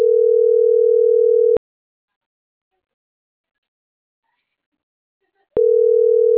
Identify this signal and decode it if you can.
Ringback tone (North America), 440 + 480 Hz.